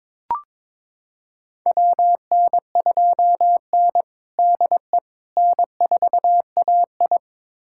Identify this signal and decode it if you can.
Morse code, heard as a keyed tone, about 22 wpm, 700 Hz. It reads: WN2N DE N4AI
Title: USA Callsign DE Callsign - 4 Characters - 22wpm